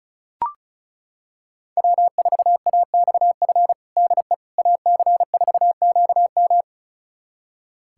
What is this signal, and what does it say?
Morse code, heard as a keyed tone, about 36 wpm, 700 Hz. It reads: W4AXF DE AC4QM
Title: USA Callsign DE Callsign - 5 Characters - 35wpm